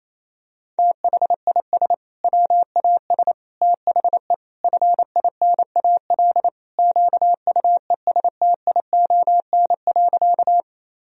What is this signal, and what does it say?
Morse code, heard as a keyed tone, about 27 wpm, 700 Hz. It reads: THIS WAS THE FINAL QUESTION.
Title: Sentences from Top 1000 Words - Review Part 2 - 28wpm